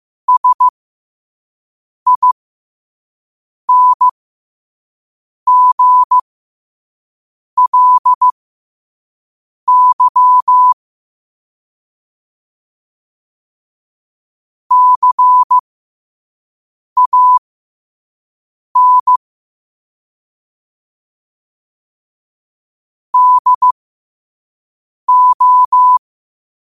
Morse code, heard as a keyed tone, about 15 wpm, spaced as for 5 wpm, 1000 Hz. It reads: SINGLY CAN DO
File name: qotdfiles / 5wpm